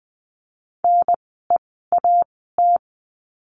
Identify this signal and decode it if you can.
Morse code, heard as a keyed tone, about 20 wpm, spaced as for 15 wpm, 700 Hz. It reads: NEAT
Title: 054 Lesson 10f - Letter r - Words 20wpm